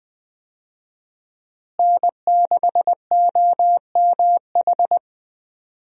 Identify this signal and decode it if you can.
Morse code, heard as a keyed tone, about 20 wpm, 700 Hz. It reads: N6OMH
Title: USA Call Signs - 5 Characters - Repeated 3x - 20wpm